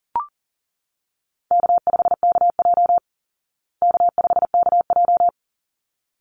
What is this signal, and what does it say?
Morse code, heard as a keyed tone, about 37 wpm, 700 Hz. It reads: K5KJ K5KJ
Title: Easy Call Signs Repeated 2x - 40wpm